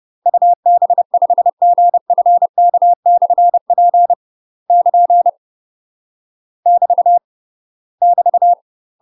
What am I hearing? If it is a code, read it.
Morse code, heard as a keyed tone, about 30 wpm, 700 Hz. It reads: UB5GFK/P <KN> <BT> <BT>
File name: UK MASTER QSOs TRAINING / 30WPM / UK MASTER QSOs 0047